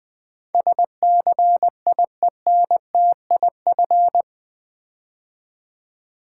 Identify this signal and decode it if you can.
Morse code, heard as a keyed tone, about 20 wpm, 700 Hz. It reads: SCIENTIF
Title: Build A Word - 4 Letters At A Time - 10-Letter Words - 20wpm